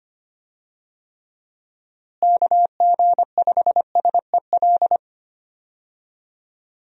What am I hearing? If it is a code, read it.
Morse code, heard as a keyed tone, about 25 wpm, 700 Hz. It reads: KG5SEL
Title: USA Call Signs - 6 Characters - Repeated 3x - 25wpm